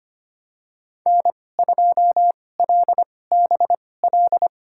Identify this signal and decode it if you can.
Morse code, heard as a keyed tone, about 25 wpm, spaced as for 18 wpm, 700 Hz. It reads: N2LBL
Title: USA Call Signs - 2x Character Spacing - 25wpm